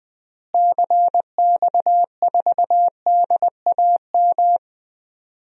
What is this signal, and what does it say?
Morse code, heard as a keyed tone, about 20 wpm, 700 Hz. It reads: CX4DAM
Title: DXCC-100 Call Signs - 6 Characters - Repeated 2x - 20wpm